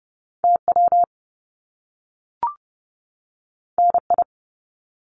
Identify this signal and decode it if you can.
Morse code, heard as a keyed tone, about 30 wpm, 700 Hz. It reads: TW NI
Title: Top Two Letter Combinations in Words 30wpm